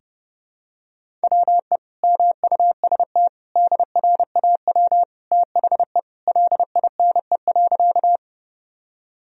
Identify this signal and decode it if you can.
Morse code, heard as a keyed tone, about 30 wpm, 700 Hz. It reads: WE MUST DRAW THE LINE.
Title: Sentences from Top 1000 Words - Review Part 1 - 30wpm